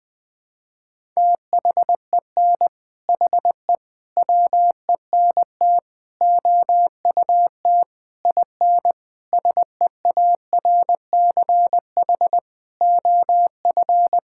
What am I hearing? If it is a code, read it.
Morse code, heard as a keyed tone, about 20 wpm, 700 Hz. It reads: THEN HE WENT OUT IN SEARCH OF
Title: Sentences from Top 1000 Words - Review Part 5 - 20wpm